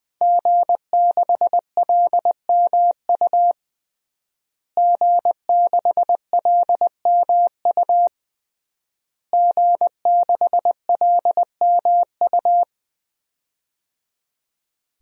Morse code, heard as a keyed tone, about 20 wpm, 700 Hz. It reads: G6LMU G6LMU G6LMU
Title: DXCC-100 Call Signs - 5 Characters - Repeated 3x - 20wpm